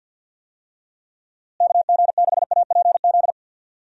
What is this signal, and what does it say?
Morse code, heard as a keyed tone, about 52 wpm, 700 Hz. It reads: KG6APZ
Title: USA Call Signs - 6 Characters - Repeated 3x - 50wpm